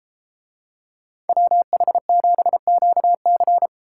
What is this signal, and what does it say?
Morse code, heard as a keyed tone, about 31 wpm, 700 Hz. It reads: WH7QC